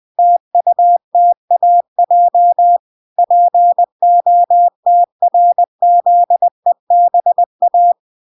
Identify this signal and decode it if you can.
Morse code, heard as a keyed tone, about 20 wpm, 700 Hz. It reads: TUTAJ POTRZEBA
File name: recordings / 20wpm0000